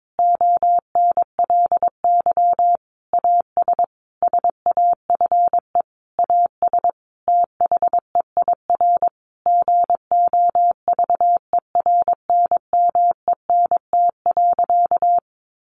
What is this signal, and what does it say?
Morse code, heard as a keyed tone, about 22 wpm, 700 Hz. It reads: ONLY AS SAFE AS THEIR GOVERNMENT.
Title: Sentences from Top 1500 Words 22wpm